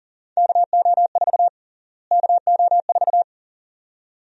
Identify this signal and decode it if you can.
Morse code, heard as a keyed tone, about 40 wpm, 700 Hz. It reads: KO4 KO4
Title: USA Call Sign Prefixes - Repeated 2x - 40wpm